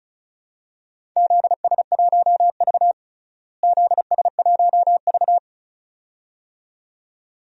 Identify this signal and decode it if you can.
Morse code, heard as a keyed tone, about 34 wpm, 700 Hz. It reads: ZS1V ZS1V
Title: DXCC-100 Call Signs - 4 Characters - Repeated 2x - 35wpm